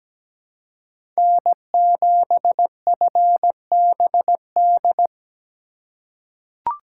Morse code, heard as a keyed tone, about 17 wpm, 700 Hz. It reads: N7FBD
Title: USA Call Signs - 5 Characters - 2x Character Spacing - 17wpm